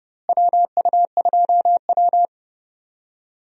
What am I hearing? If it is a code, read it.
Morse code, heard as a keyed tone, about 30 wpm, 700 Hz. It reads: WU2W
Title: USA Call Signs - 4 Characters - Repeated 3x - 30wpm